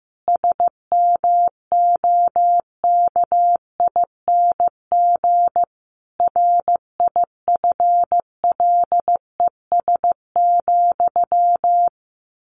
Morse code, heard as a keyed tone, about 14 wpm, 700 Hz. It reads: SMOKING RIFLES,
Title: A Sound Of Thunder 15wpm